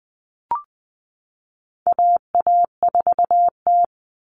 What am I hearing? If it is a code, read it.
Morse code, heard as a keyed tone, about 20 wpm, 700 Hz. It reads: AA4T
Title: All Cut Numbers - 4-Digit 20wpm